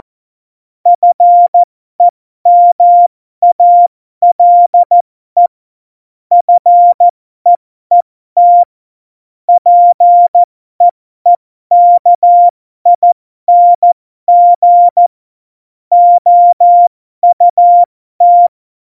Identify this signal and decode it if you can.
Morse code, heard as a keyed tone, about 14 wpm, spaced as for 12 wpm, 700 Hz. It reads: FEMALE FEET PEEKING OUT
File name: Morsekurs / Witze